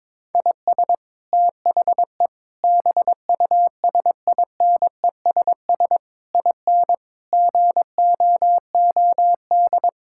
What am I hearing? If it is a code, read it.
Morse code, heard as a keyed tone, about 22 wpm, 700 Hz. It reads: IS THE BUSINESS IN GOOD
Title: Sentences from Top 1000 Words - Review Part 5 - 22wpm